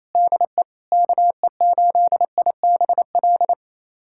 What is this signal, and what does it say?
Morse code, heard as a keyed tone, about 28 wpm, 700 Hz. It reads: DE KE8IBL
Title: USA Callsign DE Callsign - 6 Characters - 28wpm